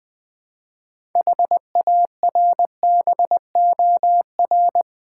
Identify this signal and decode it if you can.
Morse code, heard as a keyed tone, about 20 wpm, 700 Hz. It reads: HARBOR
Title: Build A Word - 3 Letters At A Time - 6-Letter Words - 20wpm